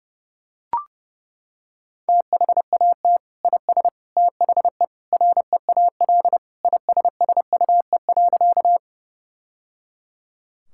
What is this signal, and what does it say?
Morse code, heard as a keyed tone, about 30 wpm, 700 Hz. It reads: THAT IS THE REAL ISSUE.
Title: Sentences from Top 1000 Words - Review Part 6 - 30wpm